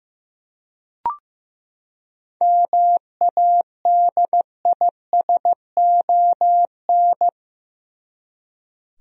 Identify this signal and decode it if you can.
Morse code, heard as a keyed tone, about 15 wpm, 700 Hz. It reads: MADISON
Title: USA Common Names - 15wpm